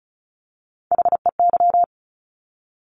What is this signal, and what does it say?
Morse code, heard as a keyed tone, about 36 wpm, 700 Hz. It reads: HEY
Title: Top Three Letter Combinations in Words 35wpm